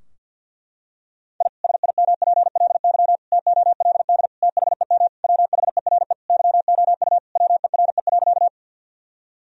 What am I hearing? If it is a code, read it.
Morse code, heard as a keyed tone, about 50 wpm, 700 Hz. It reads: I SIMPLY TOLD THEM WHERE YOU WERE.
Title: Sentences from Top 1000 Words - Review Part 4 - 50wpm